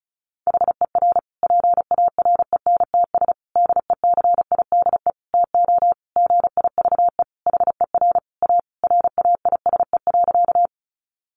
Morse code, heard as a keyed tone, about 34 wpm, 700 Hz. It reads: HER PARENTS DECIDE TO GIVE HER A RAISE.
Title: Easy Phrases 35wpm